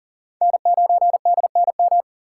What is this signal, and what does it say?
Morse code, heard as a keyed tone, about 52 wpm, 700 Hz. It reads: N9DNM